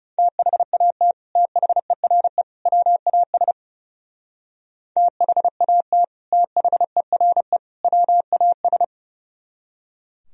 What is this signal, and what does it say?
Morse code, heard as a keyed tone, about 39 wpm, 700 Hz. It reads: THAT THERE WAS THAT THERE WAS